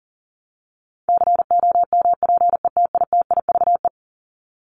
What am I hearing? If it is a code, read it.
Morse code, heard as a keyed tone, about 40 wpm, 700 Hz. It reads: COMPETITIVE
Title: Top 1001-1250 Words 40wpm